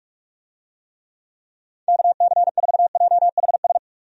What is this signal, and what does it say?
Morse code, heard as a keyed tone, about 46 wpm, 700 Hz. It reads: KC4JHS
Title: USA Call Signs - 6 Characters - 45wpm